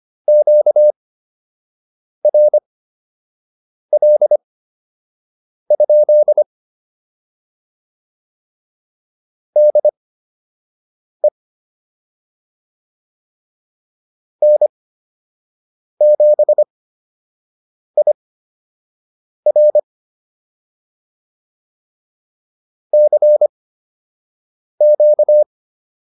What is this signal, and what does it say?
Morse code, heard as a keyed tone, about 25 wpm, spaced as for 6 wpm, 600 Hz. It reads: QRL? DE N7IR CQ